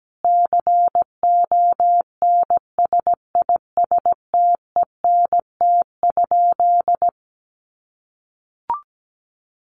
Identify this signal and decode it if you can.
Morse code, heard as a keyed tone, about 17 wpm, 700 Hz. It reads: CONSISTENT?
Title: Sentences from Top 1500 Words 17wpm